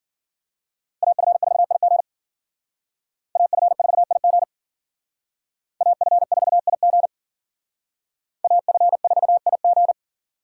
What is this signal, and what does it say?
Morse code, heard as a keyed tone, about 51 wpm, 700 Hz. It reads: AF4IG AF4IG AF4IG AF4IG